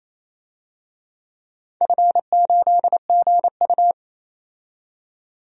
Morse code, heard as a keyed tone, about 27 wpm, 700 Hz. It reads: F8GU
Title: DXCC-100 Call Signs - 4 Characters - 28wpm